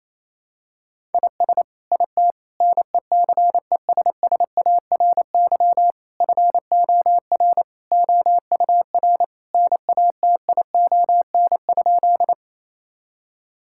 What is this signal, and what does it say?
Morse code, heard as a keyed tone, about 28 wpm, 700 Hz. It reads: IS IT NECESSARY FOR OUR NATION?
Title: Sentences from Top 1000 Words - Review Part 6 - 28wpm